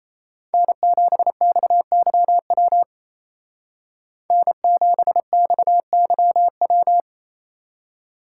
Morse code, heard as a keyed tone, about 29 wpm, 700 Hz. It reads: N7XYW N7XYW
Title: Speed Racing - USA 1x3 Call Signs 28wpm